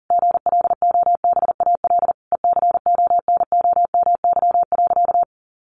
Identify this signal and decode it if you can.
Morse code, heard as a keyed tone, about 40 wpm, 700 Hz. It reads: GLOBAL ECONOMY.
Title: Sentences from Top 1200 Words 40wpm